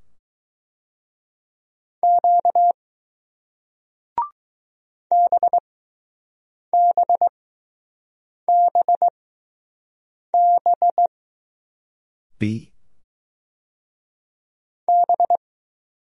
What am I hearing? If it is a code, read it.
Morse code, heard as a keyed tone, about 20 wpm, spaced as for 6 wpm, 700 Hz. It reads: Q BBBB B